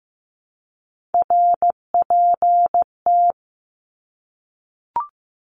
Morse code, heard as a keyed tone, about 15 wpm, 700 Hz. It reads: RPT
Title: Abbreviations Repeated 3x - 15wpm